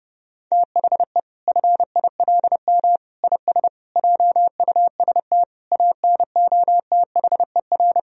Morse code, heard as a keyed tone, about 32 wpm, 700 Hz. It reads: THE FILM IS JUST ANOTHER
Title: Sentences from Top 1000 Words - Review Part 3 - 30wpm